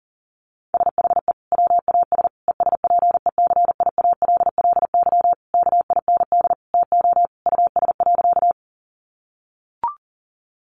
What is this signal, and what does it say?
Morse code, heard as a keyed tone, about 40 wpm, 700 Hz. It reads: SHE WAS ESPECIALLY KIND TO US.
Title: Sentences from Top 700 Words 40wpm